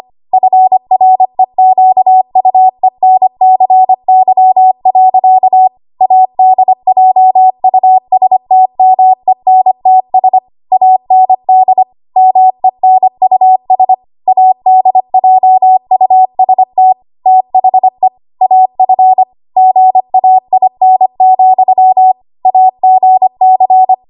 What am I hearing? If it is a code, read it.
Morse code, heard as a keyed tone, about 25 wpm, 750 Hz. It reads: FREQUENCY. ADJUSTMENTS AND MENUS ADJUST THE AF GAIN, AGC
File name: Archive / 25 WPM / 25WPM